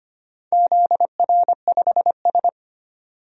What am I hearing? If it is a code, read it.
Morse code, heard as a keyed tone, about 25 wpm, 700 Hz. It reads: ZR5S
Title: DXCC-100 Call Signs - 4 Characters - 25wpm